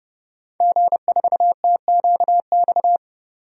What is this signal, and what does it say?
Morse code, heard as a keyed tone, about 30 wpm, 700 Hz. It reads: G4TQX
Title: DXCC-100 Call Signs - 5 Characters - 30wpm